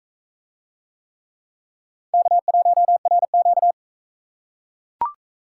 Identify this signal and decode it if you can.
Morse code, heard as a keyed tone, about 44 wpm, 700 Hz. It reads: K1RQ